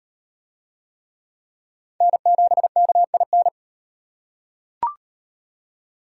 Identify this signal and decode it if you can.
Morse code, heard as a keyed tone, about 38 wpm, 700 Hz. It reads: N7KIN